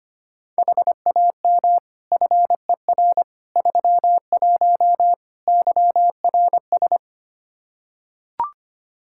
Morse code, heard as a keyed tone, about 25 wpm, 700 Hz. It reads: HAM FER 31 YRS
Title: K1USN SST QSO Practice 25wpm